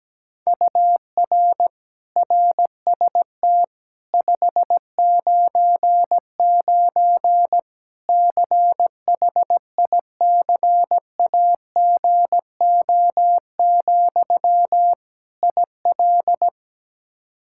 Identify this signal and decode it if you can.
Morse code, heard as a keyed tone, about 17 wpm, 700 Hz. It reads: UR RST 599 CHICAGO, IL
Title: Practice QSO - Exchange 1 - Beginning - 17wpm